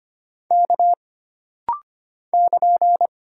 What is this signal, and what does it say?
Morse code, heard as a keyed tone, about 25 wpm, 700 Hz. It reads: K <KN>
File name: Single Character (Letter:Number:Punctuation:Prosign) - Rapid-Fire - 25wpm